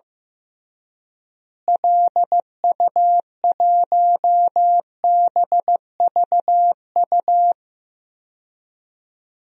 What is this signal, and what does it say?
Morse code, heard as a keyed tone, about 15 wpm, 700 Hz. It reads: LU1BVU
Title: DXCC-100 Call Signs - 6 Characters - Repeated 3x - 15wpm